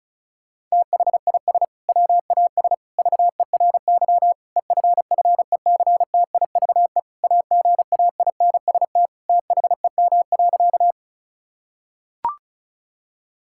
Morse code, heard as a keyed tone, about 36 wpm, 700 Hz. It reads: THIS WAS VERY EFFECTIVE AGAINST THEM.
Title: Sentences from Top 1000 Words - Review Part 6 - 35wpm